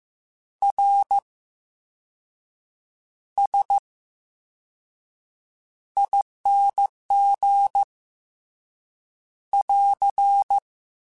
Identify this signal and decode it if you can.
Morse code, heard as a keyed tone, about 15 wpm, spaced as for 6 wpm, 800 Hz. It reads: RSING <AR>
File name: Special INGs / Special INGs-mp3s / Slow ing @13